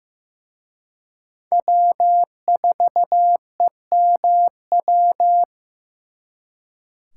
Morse code, heard as a keyed tone, about 18 wpm, 700 Hz. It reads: W4EMW
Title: Speed Racing - USA 1x3 Call Signs 15wpm